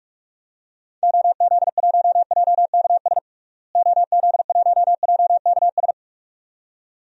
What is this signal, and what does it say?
Morse code, heard as a keyed tone, about 46 wpm, 700 Hz. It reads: OZ1JKS OZ1JKS
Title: DXCC-100 Call Signs - 6 Characters - Repeated 2x - 45wpm